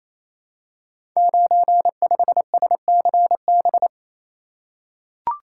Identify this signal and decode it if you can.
Morse code, heard as a keyed tone, about 29 wpm, 690 Hz. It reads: 95SCB
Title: Code Groups - 5 Letter:Numbers - 28wpm